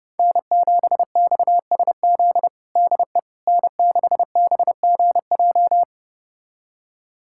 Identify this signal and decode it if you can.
Morse code, heard as a keyed tone, about 30 wpm, 700 Hz. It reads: N7XSZ DE N6BGJ
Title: USA Callsign DE Callsign - 5 Characters - 30wpm